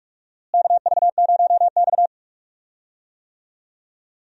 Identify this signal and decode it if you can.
Morse code, heard as a keyed tone, about 46 wpm, 700 Hz. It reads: KV0X
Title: USA Call Signs - 4 Characters - Repeated 2x - 45wpm